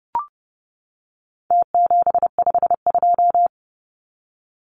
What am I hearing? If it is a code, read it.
Morse code, heard as a keyed tone, about 30 wpm, 700 Hz. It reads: T752
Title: Numbers - Contest Style - 4-Digit 30wpm